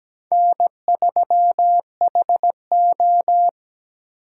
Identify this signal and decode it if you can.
Morse code, heard as a keyed tone, about 17 wpm, 700 Hz. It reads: N3HO
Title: USA Call Signs - 4 Characters - 8x Character Spacing - 17wpm